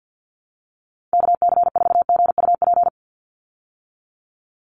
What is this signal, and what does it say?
Morse code, heard as a keyed tone, about 48 wpm, 700 Hz. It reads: KC4GUP
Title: USA Call Signs 50wpm